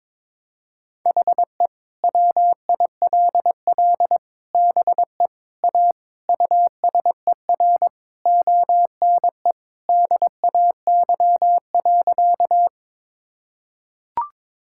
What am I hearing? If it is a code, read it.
Morse code, heard as a keyed tone, about 22 wpm, 700 Hz. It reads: HE WILL BE A USER ONE DAY.
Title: Sentences from Top 1000 Words - Review Part 5 - 22wpm